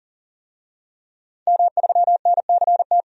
Morse code, heard as a keyed tone, about 40 wpm, 700 Hz. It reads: M3NCT
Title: DXCC-100 Call Signs - 5 Characters - 40wpm